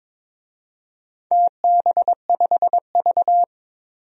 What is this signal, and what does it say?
Morse code, heard as a keyed tone, about 22 wpm, 700 Hz. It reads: TB5V
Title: DXCC-100 Call Signs - 4 Characters - 22wpm